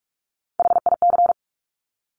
Morse code, heard as a keyed tone, about 46 wpm, 700 Hz. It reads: HIC
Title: Top Three Letter Combinations in Words 45wpm